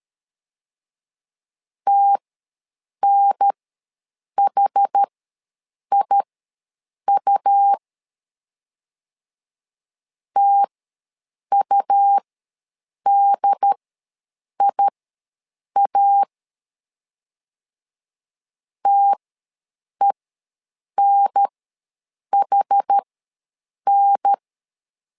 Morse code, heard as a keyed tone, about 13 wpm, spaced as for 7 wpm, 800 Hz. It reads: TNHIU TUDIA TENHN
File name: LEZIONI / 2 UHDN / 2 UHDN ws 12wpm 1708